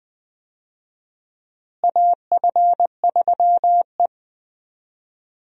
Morse code, heard as a keyed tone, about 20 wpm, 700 Hz. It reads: AF3E
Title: USA Call Signs - 4 Characters - Repeated 3x - 20wpm